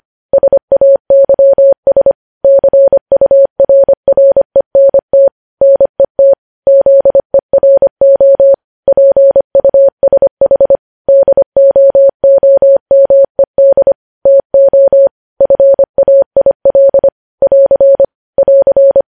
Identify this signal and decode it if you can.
Morse code, heard as a keyed tone, about 25 wpm, 550 Hz. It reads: SAYS CURRENT NET ZERO PUSH DOOMED TO FAIL <AR> <AR>
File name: podcast / News headlines-1x-25x25